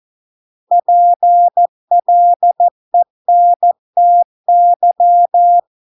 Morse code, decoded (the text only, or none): PLENTY